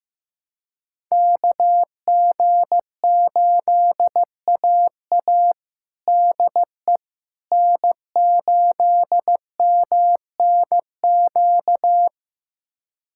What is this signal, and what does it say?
Morse code, heard as a keyed tone, about 15 wpm, 700 Hz. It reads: KG8AA DE N8MNQ
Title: USA Callsign DE Callsign - 5 Characters - 15wpm